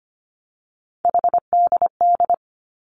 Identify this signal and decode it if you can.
Morse code, heard as a keyed tone, about 25 wpm, 700 Hz. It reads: HDD
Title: Three Challenging Letter-Number Combinations 25wpm